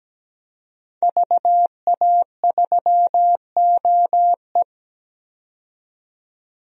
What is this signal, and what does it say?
Morse code, heard as a keyed tone, about 17 wpm, 700 Hz. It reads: VA3OE
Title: DXCC-100 Call Signs - 5 Characters - 17wpm